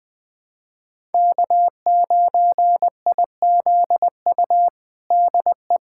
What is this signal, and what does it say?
Morse code, heard as a keyed tone, about 20 wpm, 700 Hz. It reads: K9IZU DE
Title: USA Callsign DE Callsign - 5 Characters - 20wpm